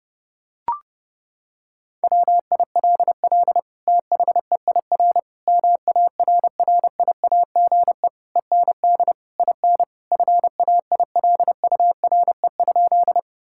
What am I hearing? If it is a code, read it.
Morse code, heard as a keyed tone, about 28 wpm, 700 Hz. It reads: WILL THEIR MARRIAGE END IN FAILURE?
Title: Sentences from Top 1000 Words - Review Part 6 - 30wpm